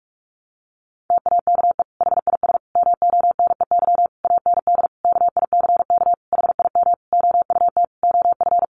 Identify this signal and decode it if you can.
Morse code, heard as a keyed tone, about 44 wpm, 700 Hz. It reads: TAKE HIS MONEY AND KICK HIM OUT OF
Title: Sentences from Top 1300 Words 45wpm